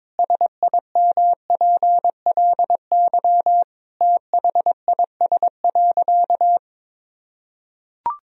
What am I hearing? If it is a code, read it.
Morse code, heard as a keyed tone, about 21 wpm, 700 Hz. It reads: SIMPLY THIS.
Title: Sentences from Top 1000 Words - Review Part 4 - 22wpm